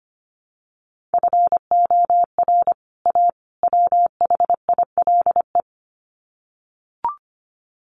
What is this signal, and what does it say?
Morse code, heard as a keyed tone, about 25 wpm, 700 Hz. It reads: FOR A WHILE
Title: Sets of 3 Words 25wpm